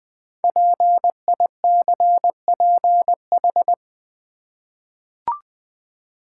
Morse code, heard as a keyed tone, about 20 wpm, 700 Hz. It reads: PICPH
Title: Code Groups - 5 Letters - 20wpm